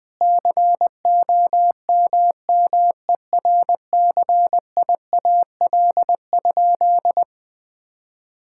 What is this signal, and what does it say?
Morse code, heard as a keyed tone, about 20 wpm, 700 Hz. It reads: COMMERCIAL?
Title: Sentences from Top 1000 Words - Review Part 5 - 20wpm